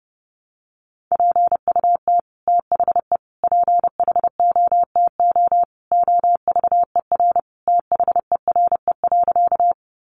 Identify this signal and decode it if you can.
Morse code, heard as a keyed tone, about 30 wpm, 700 Hz. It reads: PUT THE PHOTO OVER THERE.
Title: Sentences from Top 1200 Words 30wpm